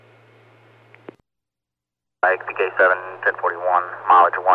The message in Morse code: none

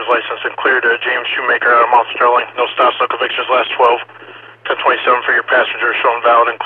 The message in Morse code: none

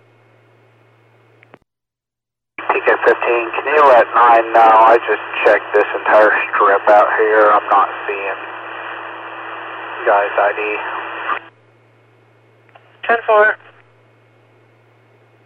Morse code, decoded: none

